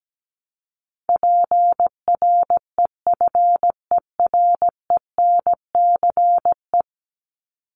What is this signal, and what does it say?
Morse code, heard as a keyed tone, about 17 wpm, 700 Hz. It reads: PREFERENCE
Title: Top 1251-1500 Words 17wpm